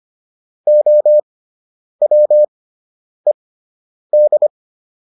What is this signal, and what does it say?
Morse code, heard as a keyed tone, about 25 wpm, spaced as for 9 wpm, 600 Hz. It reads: OWED